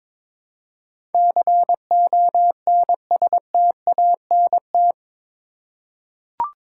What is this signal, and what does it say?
Morse code, heard as a keyed tone, about 22 wpm, 700 Hz. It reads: CONSTANT
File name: practice / 22wpm / Eight Letter Words - 22wpm